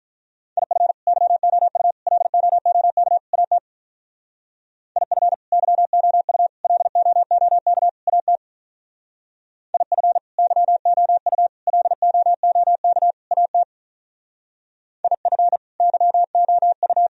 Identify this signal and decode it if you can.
Morse code, heard as a keyed tone, about 42 wpm, 700 Hz. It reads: IF YOU LOOK AT IF YOU LOOK AT IF YOU LOOK AT IF YOU
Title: Speed-Racing - Sets of 4 Words 35wpm